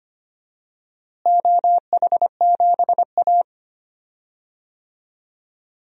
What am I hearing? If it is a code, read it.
Morse code, heard as a keyed tone, about 25 wpm, 700 Hz. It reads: OH7A
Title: DXCC-100 Call Signs - 4 Characters - Repeated 3x - 25wpm